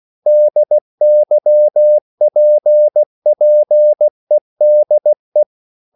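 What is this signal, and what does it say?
Morse code, heard as a keyed tone, about 16 wpm, 600 Hz. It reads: DYPPEDE
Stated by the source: Slow (QRS) Morse code at 16 wpm, from CW med Gnister.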